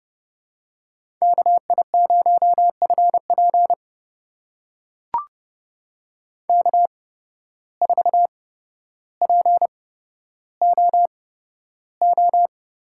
Morse code, heard as a keyed tone, about 30 wpm, spaced as for 13 wpm, 700 Hz. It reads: KI0FP K4POO
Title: USA Call Signs - 5 Characters - 8x Character Spacing - 30wpm